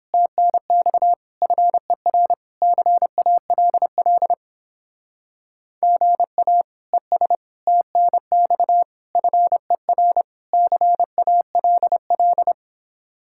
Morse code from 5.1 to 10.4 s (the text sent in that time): GA ES TNX FER